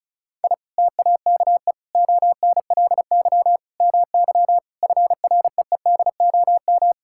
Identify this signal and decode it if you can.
Morse code, heard as a keyed tone, about 34 wpm, 700 Hz. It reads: I TAKE ONLY MY FREEDOM
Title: Sentences from Top 1000 Words - Review Part 5 - 35wpm